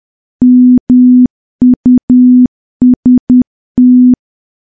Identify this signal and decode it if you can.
Morse code, heard as a keyed tone, about 10 wpm, 250 Hz. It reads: MUST
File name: morse / morse-training-10WPM